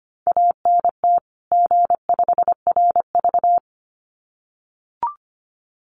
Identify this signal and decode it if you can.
Morse code, heard as a keyed tone, about 25 wpm, 700 Hz. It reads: ANT G5RV
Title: Common QSO Phrases 25wpm